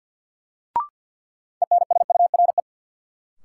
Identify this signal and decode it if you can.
Morse code, heard as a keyed tone, about 50 wpm, 700 Hz. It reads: ENSURE
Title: Sentences from Top 1100 Words - New Word Drill Alt - 50wpm